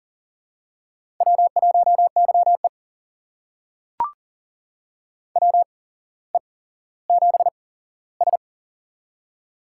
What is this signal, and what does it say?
Morse code, heard as a keyed tone, about 40 wpm, spaced as for 17 wpm, 700 Hz. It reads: W1YE WE7S
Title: USA Call Signs - 4 Characters - 8x Character Spacing - 40wpm